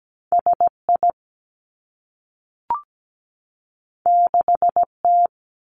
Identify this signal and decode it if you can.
Morse code, heard as a keyed tone, about 17 wpm, 700 Hz. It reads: SI 6T
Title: Two Challenging Letter-Number Combinations 17wpm